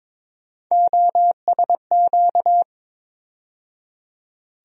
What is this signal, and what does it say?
Morse code, heard as a keyed tone, about 22 wpm, 700 Hz. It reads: OSQ
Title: USA Call Sign Postfixes - 3 Letters 22wpm